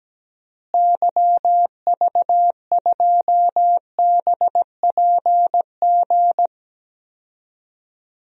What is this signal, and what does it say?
Morse code, heard as a keyed tone, about 17 wpm, 700 Hz. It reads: YV2BPG
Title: DXCC-100 Call Signs - 6 Characters - Repeated 2x - 17wpm